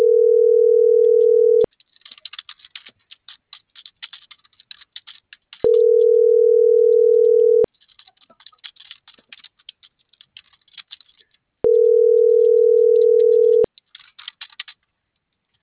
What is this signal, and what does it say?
Ringback tone (North America), 440 + 480 Hz.